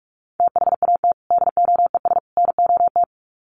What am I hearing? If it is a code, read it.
Morse code, heard as a keyed tone, about 44 wpm, 700 Hz. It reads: THAT DOES NOT
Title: Sets of 3 Words 45wpm